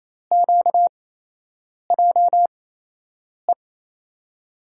Morse code, heard as a keyed tone, about 28 wpm, spaced as for 8 wpm, 700 Hz. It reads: QJE
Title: USA Call Signs - 6 Characters - 8x Character Spacing - 28wpm